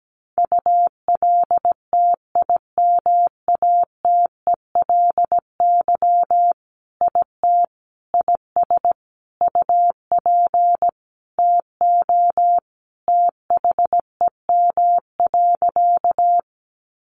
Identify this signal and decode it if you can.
Morse code, heard as a keyed tone, about 17 wpm, 700 Hz. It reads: ULTIMATELY IT IS UP TO THEM.
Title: Sentences from Top 1200 Words 17wpm